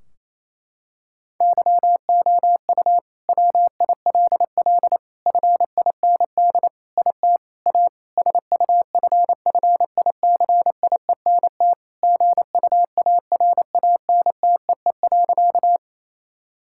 Morse code, heard as a keyed tone, about 28 wpm, 700 Hz. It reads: YOU WILL FIND IT A SUFFICIENT GUARANTEE.
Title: Sentences from Top 1600 Words 28wpm